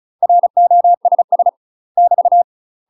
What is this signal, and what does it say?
Morse code, heard as a keyed tone, about 36 wpm, 700 Hz. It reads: ROSS <BT>
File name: UK MASTER QSOs TRAINING/35WPM